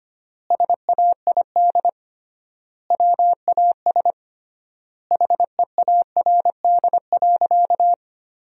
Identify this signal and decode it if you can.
Morse code, heard as a keyed tone, about 25 wpm, 700 Hz. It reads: SAID WAS HEARD.
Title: 3x Word-Spacing - Sentences from Top 400 Words - 25wpm